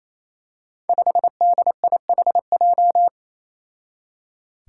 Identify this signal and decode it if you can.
Morse code, heard as a keyed tone, about 28 wpm, 700 Hz. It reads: 5DIHJ
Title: Five Challenging Letter-Number Combinations 28wpm